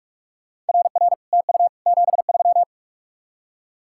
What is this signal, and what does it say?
Morse code, heard as a keyed tone, about 56 wpm, 700 Hz. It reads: RR TU 73
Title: Speed-Racing Common QSO Phrases Encore - 45wpm